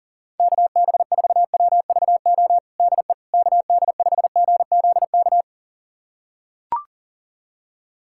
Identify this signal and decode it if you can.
Morse code, heard as a keyed tone, about 40 wpm, 700 Hz. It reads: KB4WVO DE KD5GZK
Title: USA Callsign DE Callsign 40wpm